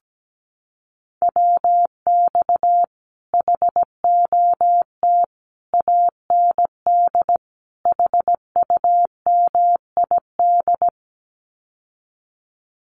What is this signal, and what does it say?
Morse code, heard as a keyed tone, about 17 wpm, 700 Hz. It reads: WX HOT AND HUMID
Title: Common QSO Phrases Encore 17wpm